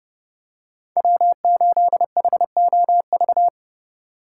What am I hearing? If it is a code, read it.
Morse code, heard as a keyed tone, about 30 wpm, 700 Hz. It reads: W8HOV
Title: USA Call Signs - 5 Characters - 6x Character Spacing - 30wpm